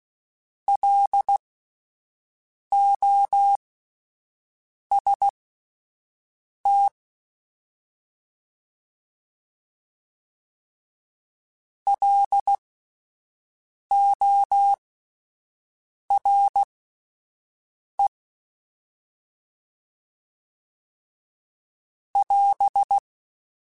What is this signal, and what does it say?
Morse code, heard as a keyed tone, about 16 wpm, spaced as for 5 wpm, 800 Hz. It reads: LOST LORE <AS>
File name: Alphabetical / Alphabetical mp3s / Ls @5